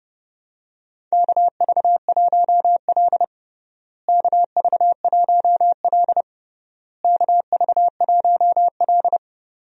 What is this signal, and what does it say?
Morse code, heard as a keyed tone, about 30 wpm, 700 Hz. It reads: KV1L KV1L KV1L
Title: USA Call Signs - 4 Characters - Repeated 3x - 30wpm